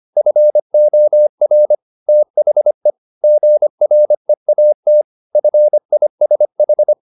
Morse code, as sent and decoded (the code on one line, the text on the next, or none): ..-. --- .-. / - .... / --. .-. .- - / ..-. .. ... ....
FOR THE GREAT FISH